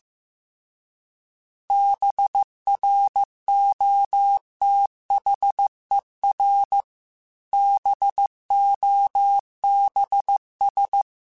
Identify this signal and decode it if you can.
Morse code, heard as a keyed tone, about 15 wpm, 800 Hz. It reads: BROTHER BOBS